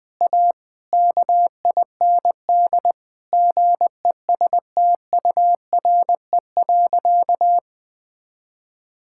Morse code, heard as a keyed tone, about 21 wpm, 700 Hz. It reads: A KIND GESTURE.
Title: Difficult Sentences with Hints - 20wpm